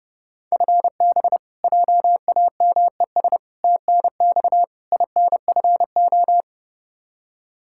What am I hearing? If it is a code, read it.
Morse code, heard as a keyed tone, about 29 wpm, 700 Hz. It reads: FB JAMES TNX INFO
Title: Practice QSO - Exchange 3 - More Info - 30wpm